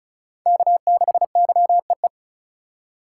Morse code, heard as a keyed tone, about 36 wpm, 700 Hz. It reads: K6YEE
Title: USA Call Signs - 5 Characters - 2x Character Spacing - 35wpm